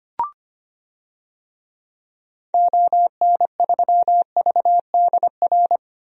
Morse code, heard as a keyed tone, about 25 wpm, 700 Hz. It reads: ON3VDR
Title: DXCC-100 Call Signs - 6 Characters - Repeated 2x - 25wpm